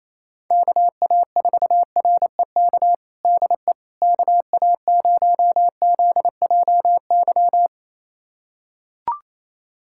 Morse code, heard as a keyed tone, about 28 wpm, 700 Hz. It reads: KA4REK DE KA0ZJY
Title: USA Callsign DE Callsign 28wpm